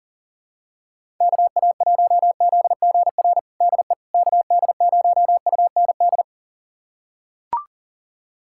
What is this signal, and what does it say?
Morse code, heard as a keyed tone, about 40 wpm, 700 Hz. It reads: KA1ZGR DE KD0UND
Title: USA Callsign DE Callsign - 6 Characters - 40wpm